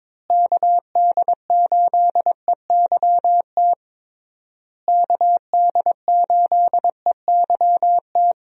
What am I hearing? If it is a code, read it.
Morse code, heard as a keyed tone, about 22 wpm, 700 Hz. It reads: KD8EYT KD8EYT
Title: USA Call Signs - 6 Characters - Repeated 3x - 22wpm